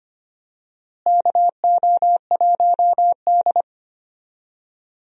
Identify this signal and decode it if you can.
Morse code, heard as a keyed tone, about 25 wpm, 700 Hz. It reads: KO1D
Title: USA Call Signs - 4 Characters - Repeated 2x - 25wpm